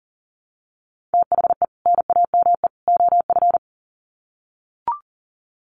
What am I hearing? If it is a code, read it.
Morse code, heard as a keyed tone, about 40 wpm, 700 Hz. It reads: THE NAME OF
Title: Sets of 3 Words 40wpm